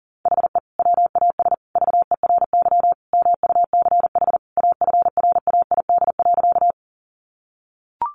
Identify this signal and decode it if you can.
Morse code, heard as a keyed tone, about 40 wpm, 700 Hz. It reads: HE WAS VERY MUCH AFRAID.
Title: Sentences from Top 1500 Words 40wpm